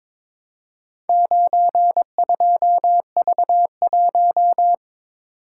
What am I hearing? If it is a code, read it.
Morse code, heard as a keyed tone, about 22 wpm, 700 Hz. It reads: 92V1
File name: practice / 22wpm / Four Letter-Number Combinations - 22wpm